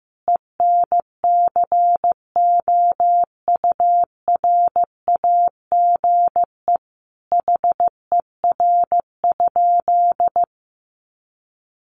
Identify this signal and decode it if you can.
Morse code, heard as a keyed tone, about 15 wpm, 700 Hz. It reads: ENCOURAGE HER?
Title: Sentences from Top 1400 Words 15wpm